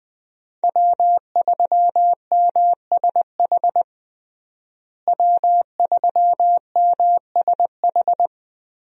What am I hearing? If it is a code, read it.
Morse code, heard as a keyed tone, about 20 wpm, 700 Hz. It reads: W3MSH W3MSH
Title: USA Call Signs - 5 Characters - Repeated 3x - 20wpm